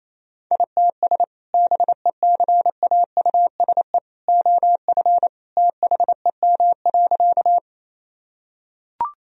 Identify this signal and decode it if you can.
Morse code, heard as a keyed tone, about 28 wpm, 700 Hz. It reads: ITS BECAUSE OF THEM.
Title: Sentences from Top 1000 Words - Review Part 3 - 28wpm